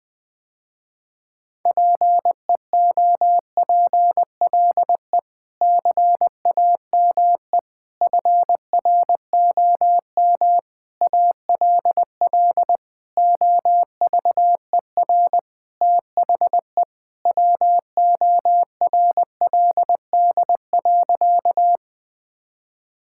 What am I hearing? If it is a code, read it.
Morse code, heard as a keyed tone, about 21 wpm, 700 Hz. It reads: PEOPLE CAME FROM ALL OVER THE WORLD.
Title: Sentences with 5-Word Phrases - Spoken Beginning - 20wpm